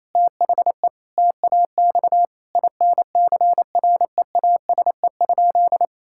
Morse code, heard as a keyed tone, about 28 wpm, 700 Hz. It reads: THE TAX INCREASE?
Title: Sentences from Top 1000 Words - Review Part 5 - 28wpm